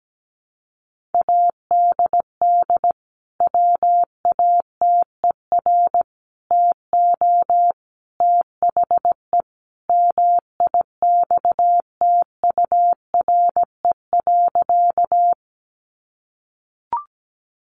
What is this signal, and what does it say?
Morse code, heard as a keyed tone, about 17 wpm, 700 Hz. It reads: ADD WATER TO THE MIXTURE.
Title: Sentences from Top 1500 Words 17wpm